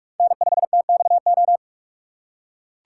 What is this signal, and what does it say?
Morse code, heard as a keyed tone, about 44 wpm, 690 Hz. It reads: N5TXO
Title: USA Call Signs - 5 Characters - 2x Character Spacing - 45wpm